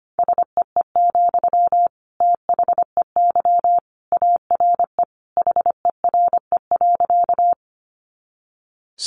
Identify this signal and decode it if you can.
Morse code, heard as a keyed tone, about 25 wpm, 700 Hz. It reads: SEE, THEY ARE HERE.
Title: Sentences from Top 200 Words 25wpm